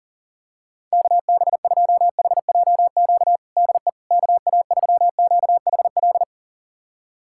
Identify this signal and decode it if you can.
Morse code, heard as a keyed tone, about 40 wpm, 700 Hz. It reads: KB2HJQ DE KA3QHL
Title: USA Callsign DE Callsign 40wpm